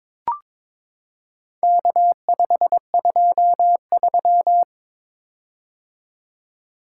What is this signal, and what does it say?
Morse code, heard as a keyed tone, about 22 wpm, 700 Hz. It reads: K523
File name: practice / 22wpm / Four Letter-Number Combinations - 22wpm